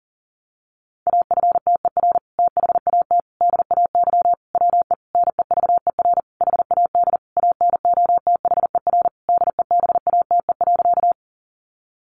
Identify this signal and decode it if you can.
Morse code, heard as a keyed tone, about 37 wpm, 700 Hz. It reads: AFTER THAT DAY WE NEVER HAD ANOTHER DEBATE.
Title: Sentences from Top 1100 Words 40wpm